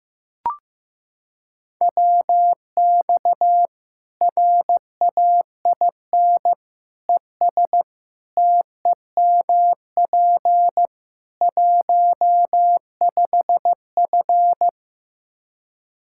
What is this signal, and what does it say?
Morse code, heard as a keyed tone, about 15 wpm, 700 Hz. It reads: WX RAIN ES TEMP 15F
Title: Practice QSO - Exchange 2 - Info - 15wpm